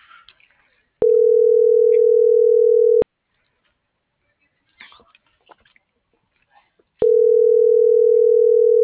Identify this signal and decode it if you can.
Ringback tone (North America), 440 + 480 Hz.
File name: CENTRAL SAGOT / out xquenda.sagot - .85587